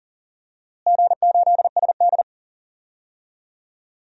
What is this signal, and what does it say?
Morse code, heard as a keyed tone, about 40 wpm, 700 Hz. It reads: G8SD
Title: DXCC-100 Call Signs - 4 Characters - Repeated 2x - 40wpm